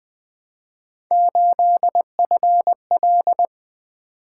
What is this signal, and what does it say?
Morse code, heard as a keyed tone, about 20 wpm, 700 Hz. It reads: 8FL
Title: Three Letter-Number Combinations - 20wpm